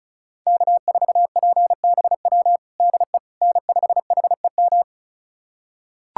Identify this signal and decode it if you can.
Morse code, heard as a keyed tone, about 36 wpm, 690 Hz. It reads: K4PBW DE N5HEM
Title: USA Callsign DE Callsign - 5 Characters - 35wpm